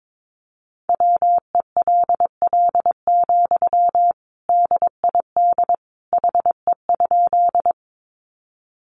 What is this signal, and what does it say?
Morse code, heard as a keyed tone, about 22 wpm, 700 Hz. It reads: WELL, DID HE?
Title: Sentences from Top 200 Words 22wpm